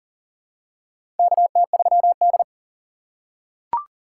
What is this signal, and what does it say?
Morse code, heard as a keyed tone, about 40 wpm, 700 Hz. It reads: KT3D
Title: USA Call Signs - 4 Characters - 3x Character Spacing - 40wpm